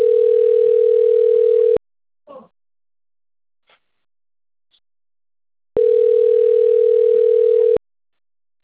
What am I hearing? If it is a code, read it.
Ringback tone (North America), 440 + 480 Hz.